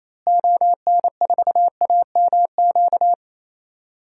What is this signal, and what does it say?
Morse code, heard as a keyed tone, about 28 wpm, 700 Hz. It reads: ON4AMQ
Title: DXCC-100 Call Signs - 6 Characters - Repeated 2x - 28wpm